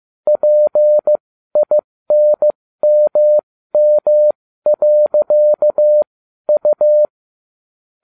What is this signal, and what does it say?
Morse code, heard as a keyed tone, about 15 wpm, spaced as for 12 wpm, 600 Hz. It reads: PINMM.U